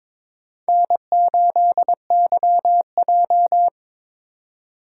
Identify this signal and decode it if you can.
Morse code, heard as a keyed tone, about 21 wpm, 700 Hz. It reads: N8YJ
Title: USA Call Signs - 4x Character Spacing - 22wpm